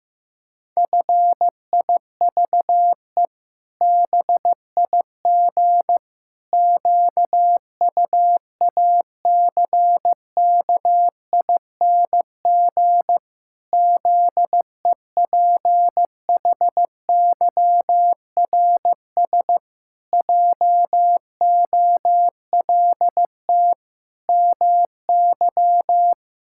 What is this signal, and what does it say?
Morse code, heard as a keyed tone, about 15 wpm, 700 Hz. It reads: FIVE BIG QUACKING ZEPHYRS JOLT MY
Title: Pangram Challenge v2 - 15wpm